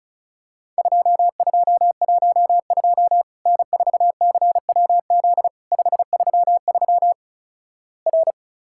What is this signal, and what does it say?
Morse code, heard as a keyed tone, about 36 wpm, 700 Hz. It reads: 2212 N4CWZ 533 R
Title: Worked All Europe DX Contest - QTC Exchange - 35wpm